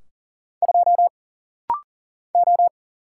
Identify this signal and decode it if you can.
Morse code, heard as a keyed tone, about 39 wpm, 700 Hz. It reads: 2 O
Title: Single Letter-Number - Rapid-Fire 40wpm